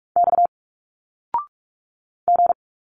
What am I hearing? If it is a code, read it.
Morse code, heard as a keyed tone, about 44 wpm, 700 Hz. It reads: X G
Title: Single Letters 45wpm